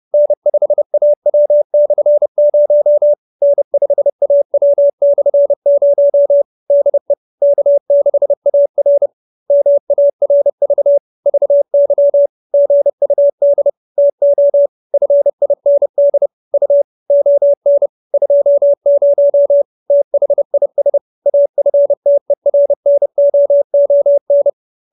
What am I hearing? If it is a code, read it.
Morse code, heard as a keyed tone, about 30 wpm, 570 Hz. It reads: N5AW/0 N5AW/0 DE K6AR MARV VY GUD TO FIND U ON 20 THIS AFTERNOON